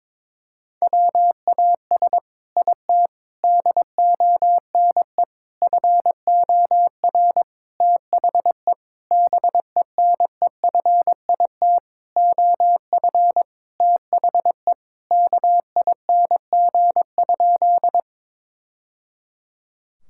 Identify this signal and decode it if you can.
Morse code, heard as a keyed tone, about 22 wpm, 700 Hz. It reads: WAS IT DONE FOR THE BENEFIT OF THE KING?
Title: Sentences from Top 1000 Words - Review Part 6 - 22wpm